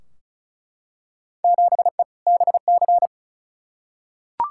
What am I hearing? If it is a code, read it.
Morse code, heard as a keyed tone, about 36 wpm, 690 Hz. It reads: 7E BC
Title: Field Day Exchanges Encore - 35wpm